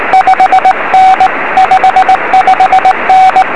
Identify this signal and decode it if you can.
Morse code, heard as a keyed tone, about 19 wpm, 760 Hz, with radio noise. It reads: 5 N 55N
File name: 0345z dl4dtu 3cm